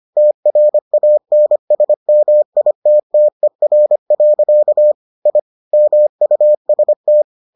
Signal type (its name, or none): Morse code